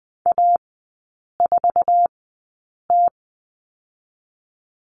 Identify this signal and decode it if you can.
Morse code, heard as a keyed tone, about 20 wpm, spaced as for 8 wpm, 700 Hz. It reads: A4T
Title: 047 Lesson 9e - Number 4 - Three Characters 20wpm